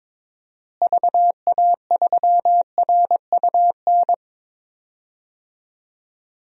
Morse code, heard as a keyed tone, about 22 wpm, 700 Hz. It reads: VA3RUN
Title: DXCC-100 Call Signs - 6 Characters - Repeated 3x - 22wpm